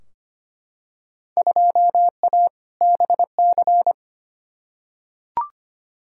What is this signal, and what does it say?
Morse code, heard as a keyed tone, about 25 wpm, 700 Hz. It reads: 2A BC
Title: Field Day Exchanges Encore - 25wpm